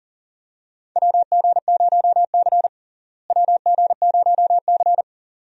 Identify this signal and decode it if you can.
Morse code, heard as a keyed tone, about 40 wpm, 700 Hz. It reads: WG0C WG0C
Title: USA Call Signs - 4 Characters - Repeated 2x - 40wpm